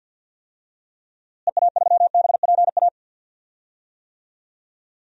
Morse code, heard as a keyed tone, about 52 wpm, 700 Hz. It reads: EA3BPA
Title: DXCC-100 Call Signs - 6 Characters - Repeated 2x - 50wpm